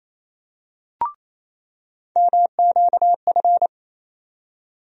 Morse code, heard as a keyed tone, about 28 wpm, 700 Hz. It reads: MQF
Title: USA Call Sign Postfixes - 3 Letters 28wpm